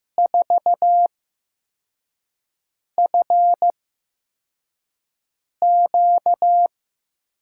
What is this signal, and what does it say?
Morse code, heard as a keyed tone, about 15 wpm, spaced as for 4 wpm, 700 Hz. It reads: 4FQ
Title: USA Call Signs - 6 Characters - 8x Character Spacing - 15wpm